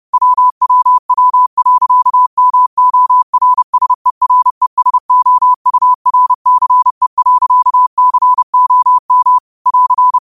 Morse code, heard as a keyed tone, about 30 wpm, 1000 Hz. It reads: WWW.MORSERESOURCE.COM <AR>
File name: qotdfiles / 30wpm